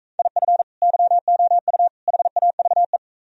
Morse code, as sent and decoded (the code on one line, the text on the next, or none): .. ..-. / -.-- --- ..- / .... .- ...- .
IF YOU HAVE